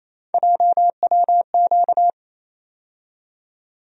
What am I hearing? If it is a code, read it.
Morse code, heard as a keyed tone, about 28 wpm, 700 Hz. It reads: JWQ
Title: USA Call Sign Postfixes - 3 Letters 28wpm